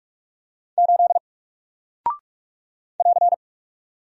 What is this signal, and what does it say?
Morse code, heard as a keyed tone, about 44 wpm, 700 Hz. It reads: 8 <AR>
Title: Single Character (Letter:Number:Punctuation:Prosign) 45wpm